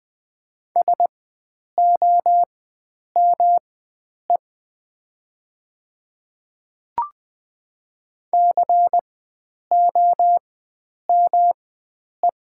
Morse code, heard as a keyed tone, about 20 wpm, spaced as for 9 wpm, 700 Hz. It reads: SOME COME
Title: 4x Character Spacing - Top 100 Words - 20wpm